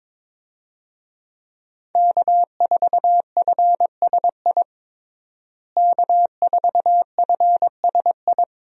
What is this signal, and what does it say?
Morse code, heard as a keyed tone, about 22 wpm, 700 Hz. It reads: K4FSI K4FSI
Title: USA Call Signs - 5 Characters - Repeated 2x - 22wpm